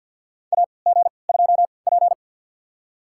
Morse code, heard as a keyed tone, about 49 wpm, spaced as for 28 wpm, 700 Hz. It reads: AG2P
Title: USA Call Signs - 4 Characters - 3x Character Spacing - 50wpm